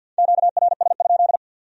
Morse code, heard as a keyed tone, about 62 wpm, 700 Hz. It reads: QRS?
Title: Speed-Racing Q-Codes - 50wpm